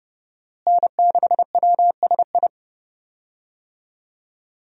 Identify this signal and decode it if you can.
Morse code, heard as a keyed tone, about 32 wpm, 700 Hz. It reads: N6WSI
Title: USA Call Signs - 5 Characters - Repeated 2x - 30wpm